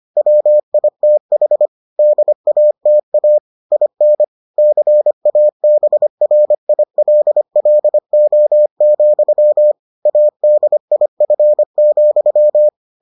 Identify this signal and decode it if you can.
Morse code, heard as a keyed tone, about 25 wpm, 600 Hz. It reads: WITH DATA IN CABRILLO, ADIF,